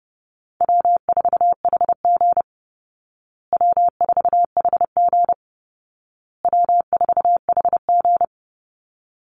Morse code, heard as a keyed tone, about 30 wpm, 700 Hz. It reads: W4HG W4HG W4HG
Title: Call Signs Repeated 2x - 30wpm